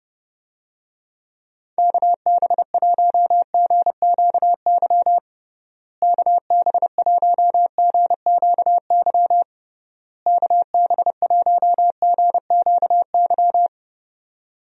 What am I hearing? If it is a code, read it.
Morse code, heard as a keyed tone, about 30 wpm, 700 Hz. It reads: KB1GQY KB1GQY KB1GQY
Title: USA Call Signs - 6 Characters - Repeated 3x - 30wpm